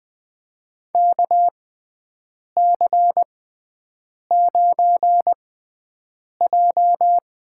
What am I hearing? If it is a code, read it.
Morse code, heard as a keyed tone, about 20 wpm, spaced as for 7 wpm, 700 Hz. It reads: KC9J
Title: USA Call Signs - 6 Characters - 6x Character Spacing - 20wpm